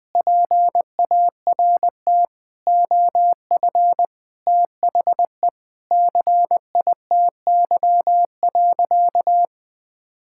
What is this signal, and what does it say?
Morse code, heard as a keyed tone, about 21 wpm, 700 Hz. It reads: PART OF THE CITY.
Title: Sentences from Top 1000 Words - Review Part 5 - 20wpm